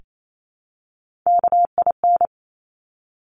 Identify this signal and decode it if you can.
Morse code, heard as a keyed tone, about 28 wpm, 700 Hz. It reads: KIN
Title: Top Three Letter Combinations in Words 28wpm